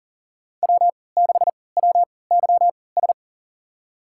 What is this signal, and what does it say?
Morse code, heard as a keyed tone, about 38 wpm, spaced as for 22 wpm, 700 Hz. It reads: W6WYS
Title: USA Call Signs - 3x Character Spacing - 40wpm